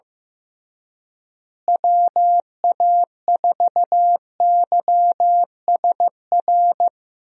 Morse code, heard as a keyed tone, about 16 wpm, 700 Hz. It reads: WA4YSR
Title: USA Call Signs - 6 Characters - Repeated 3x - 15wpm